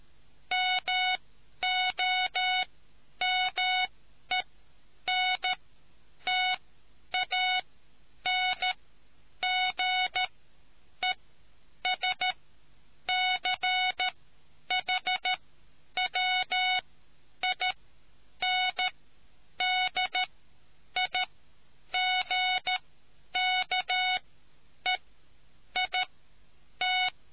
Morse code, heard as a keyed tone, about 13 wpm, spaced as for 8 wpm, 2200 Hz. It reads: MOMENTANGESCHWINDIGKEIT